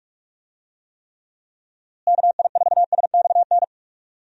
Morse code, heard as a keyed tone, about 41 wpm, 700 Hz. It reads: KI4SXN
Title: USA Call Signs - 6 Characters - Repeated 3x - 45wpm